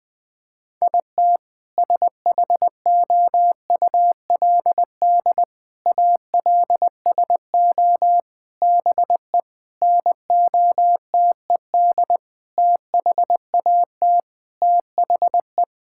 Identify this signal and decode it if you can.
Morse code, heard as a keyed tone, about 20 wpm, 700 Hz. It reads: IT SHOULD ALSO BE NOTED THAT THE
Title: Sentences with 6-Word Phrases - 20wpm